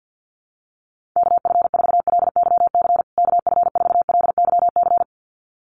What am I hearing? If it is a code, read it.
Morse code, heard as a keyed tone, about 50 wpm, 700 Hz. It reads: KF4LYC KF4LYC
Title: USA Call Signs Repeated 2x 50wpm